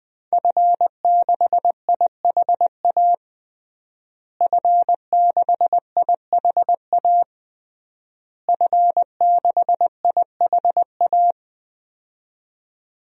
Morse code, heard as a keyed tone, about 20 wpm, 700 Hz. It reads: F6IHA F6IHA F6IHA
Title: DXCC-100 Call Signs - 5 Characters - Repeated 3x - 20wpm